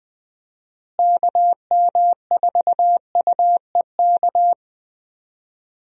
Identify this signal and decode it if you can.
Morse code, heard as a keyed tone, about 21 wpm, 700 Hz. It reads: KM4UEK
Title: USA Call Signs - 6 Characters - Repeated 3x - 20wpm